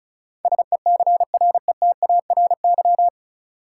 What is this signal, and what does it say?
Morse code, heard as a keyed tone, about 34 wpm, 700 Hz. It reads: SECRETARY
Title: Sentences from Top 1300 Words - New Word Drill Alt - 35wpm